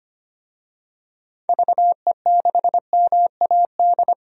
Morse code, heard as a keyed tone, about 25 wpm, 700 Hz. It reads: VE6MAD
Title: DXCC-100 Call Signs - 6 Characters - 25wpm